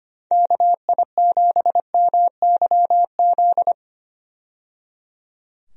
Morse code, heard as a keyed tone, about 25 wpm, 700 Hz. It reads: KI7MYZ
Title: USA Call Signs - 6 Characters - 25wpm